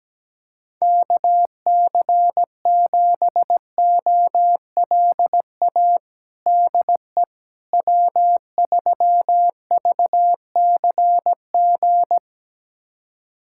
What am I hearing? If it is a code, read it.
Morse code, heard as a keyed tone, about 17 wpm, 700 Hz. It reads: KC7OLA DE W3VCG
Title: USA Callsign DE Callsign 17wpm